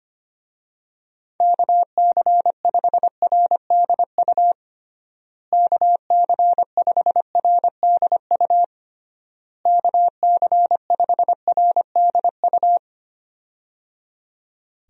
Morse code, heard as a keyed tone, about 25 wpm, 700 Hz. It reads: KC5RDU KC5RDU KC5RDU
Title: USA Call Signs - 6 Characters - Repeated 3x - 25wpm